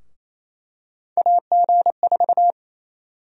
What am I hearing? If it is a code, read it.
Morse code, heard as a keyed tone, about 28 wpm, 700 Hz. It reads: AG4
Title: USA Call Sign Prefixes - Repeated 2x - 28wpm